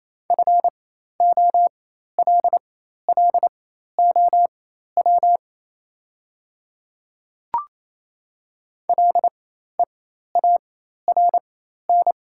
Morse code, heard as a keyed tone, about 28 wpm, spaced as for 13 wpm, 700 Hz. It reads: FOLLOW LEARN
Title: 4x Character Spacing - Top 300 Words - 28wpm